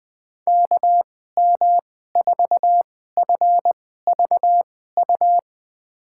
Morse code, heard as a keyed tone, about 21 wpm, spaced as for 15 wpm, 700 Hz. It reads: KM4FVU
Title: USA Call Signs - 6 Characters - 2x Character Spacing - 20wpm